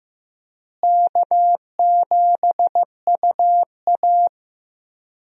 Morse code, heard as a keyed tone, about 15 wpm, 700 Hz. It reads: K7UA
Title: USA Call Signs - 4 Characters - Repeated 3x - 15wpm